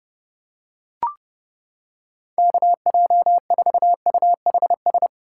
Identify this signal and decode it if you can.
Morse code, heard as a keyed tone, about 31 wpm, 700 Hz. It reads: KJ4UHS